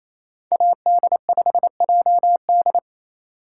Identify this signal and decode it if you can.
Morse code, heard as a keyed tone, about 28 wpm, 700 Hz. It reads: AD5JD
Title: USA Call Signs - 5 Characters - 8x Character Spacing - 28wpm